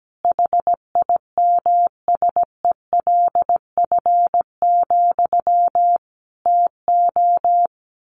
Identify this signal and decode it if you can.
Morse code, heard as a keyed tone, about 17 wpm, 700 Hz. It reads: HIMSELF, TO
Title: A Sound Of Thunder 17wpm